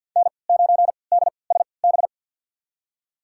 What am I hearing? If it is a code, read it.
Morse code, heard as a keyed tone, about 52 wpm, spaced as for 28 wpm, 700 Hz. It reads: N9DSB
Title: USA Call Signs - 5 Characters - 3x Character Spacing - 50wpm